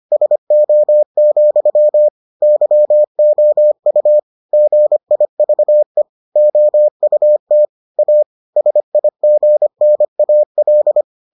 Morse code, heard as a keyed tone, about 25 wpm, 600 Hz. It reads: SO, YOU GIVE OUT A SIGNAL